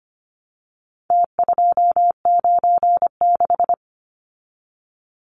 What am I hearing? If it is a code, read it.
Morse code, heard as a keyed tone, about 25 wpm, 700 Hz. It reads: T296
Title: Numbers - Contest Style - 4-Digit 25wpm